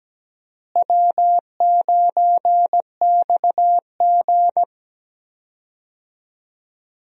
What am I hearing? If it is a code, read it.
Morse code, heard as a keyed tone, about 17 wpm, 700 Hz. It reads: W9XG
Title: USA Call Signs - 4 Characters - Repeated 2x - 17wpm